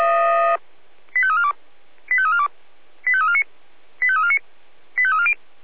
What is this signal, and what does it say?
Five-tone selective call (ZVEI), heard as quick runs of five tones, tones of about 70 ms. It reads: 85342 85342 85349 85349 85340